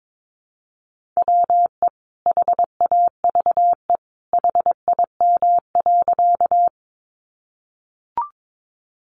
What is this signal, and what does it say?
Morse code, heard as a keyed tone, about 22 wpm, 700 Hz. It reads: WE HAVE HIM.
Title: Sentences from Top 100 Words 22wpm